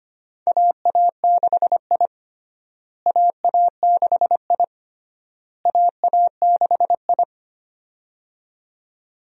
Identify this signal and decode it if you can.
Morse code, heard as a keyed tone, about 25 wpm, 700 Hz. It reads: AA6I AA6I AA6I
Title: USA Call Signs - 4 Characters - Repeated 3x - 25wpm